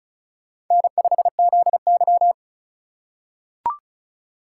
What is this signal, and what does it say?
Morse code, heard as a keyed tone, about 36 wpm, 700 Hz. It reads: N5ZY
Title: USA Call Signs - 3x Character Spacing - 35wpm